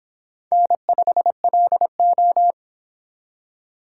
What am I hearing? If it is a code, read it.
Morse code, heard as a keyed tone, about 25 wpm, 700 Hz. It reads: N5LO